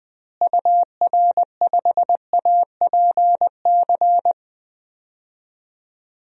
Morse code, heard as a keyed tone, about 21 wpm, 700 Hz. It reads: UR5APC
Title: DXCC-100 Call Signs - 6 Characters - Repeated 2x - 20wpm